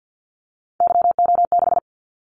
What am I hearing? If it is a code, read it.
Morse code, heard as a keyed tone, about 50 wpm, 700 Hz. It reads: YO6
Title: Callsign Prefixes for Top 25 DXCC - 50wpm